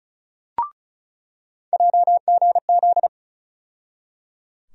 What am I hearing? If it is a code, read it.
Morse code, heard as a keyed tone, about 36 wpm, 700 Hz. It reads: JGZ
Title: USA Call Sign Postfixes - 3 Letters 35wpm